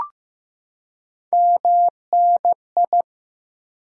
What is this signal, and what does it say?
Morse code, heard as a keyed tone, about 15 wpm, 700 Hz. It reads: MNI
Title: Top 90 QSO Elements 15wpm